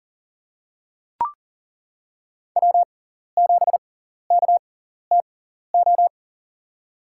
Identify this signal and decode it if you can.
Morse code, heard as a keyed tone, about 40 wpm, spaced as for 14 wpm, 700 Hz. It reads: W7KTO
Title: USA Call Signs - 6x Character Spacing - 40wpm